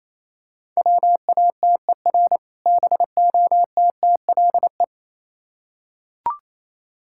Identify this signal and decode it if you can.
Morse code, heard as a keyed tone, about 27 wpm, 700 Hz. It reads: WATER BOTTLE
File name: Easy Words - Speed-Racing with Hints - 22wpm